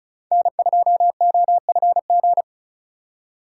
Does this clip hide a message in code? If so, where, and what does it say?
Yes: Morse code from 0.3 to 2.4 s, heard as a keyed tone, about 45 wpm, 700 Hz. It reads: N2OFG